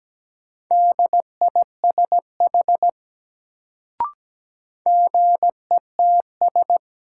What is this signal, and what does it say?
Morse code, heard as a keyed tone, about 17 wpm, 700 Hz. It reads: DISH GETS
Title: Sentences from Top 1300 Words - New Word Drill - 17wpm